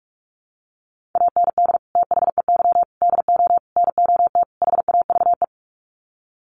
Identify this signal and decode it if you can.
Morse code, heard as a keyed tone, about 44 wpm, 700 Hz. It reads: AND THEY DO NOT HAVE
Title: Sets of 5 Words Encore2 45wpm